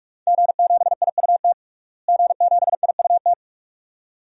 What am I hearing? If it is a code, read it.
Morse code, heard as a keyed tone, about 46 wpm, 700 Hz. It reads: G7IUT G7IUT
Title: DXCC-100 Call Signs - 5 Characters - Repeated 2x - 45wpm